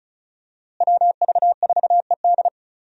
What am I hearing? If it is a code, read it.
Morse code, heard as a keyed tone, about 36 wpm, 700 Hz. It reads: WV4ED
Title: USA Call Signs - 5 Characters - Repeated 2x - 35wpm